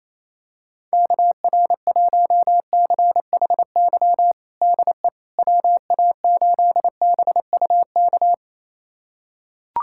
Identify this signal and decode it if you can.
Morse code, heard as a keyed tone, about 28 wpm, 700 Hz. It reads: KR1CHY DE WA8BUK
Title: USA Callsign DE Callsign - 6 Characters - 28wpm